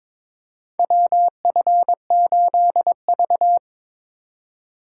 Morse code, heard as a keyed tone, about 22 wpm, 700 Hz. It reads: WF8V
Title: USA Call Signs - 4 Characters - 2x Character Spacing - 22wpm